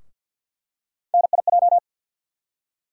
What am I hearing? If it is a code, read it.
Morse code, heard as a keyed tone, about 50 wpm, 700 Hz. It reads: NIJ
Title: USA Call Sign Postfixes - 3 Letters 50wpm